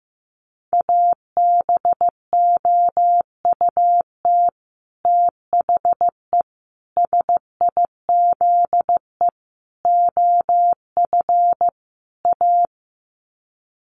Morse code, heard as a keyed tone, about 15 wpm, 700 Hz. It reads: ABOUT THE SIZE OF A
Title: Sets of 5 Words 15wpm